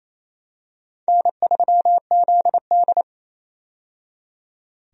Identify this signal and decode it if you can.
Morse code, heard as a keyed tone, about 28 wpm, 700 Hz. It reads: N3ZD
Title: USA Call Signs - 4 Characters - 28wpm